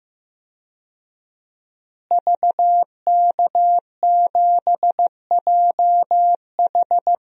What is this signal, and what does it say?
Morse code, heard as a keyed tone, about 15 wpm, 700 Hz. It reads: VK7JH
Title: DXCC-100 Call Signs - 5 Characters - Repeated 2x - 15wpm